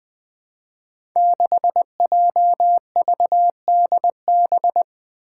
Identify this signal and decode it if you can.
Morse code, heard as a keyed tone, about 20 wpm, 700 Hz. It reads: 6JVDB
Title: Five Challenging Letter-Number Combinations 20wpm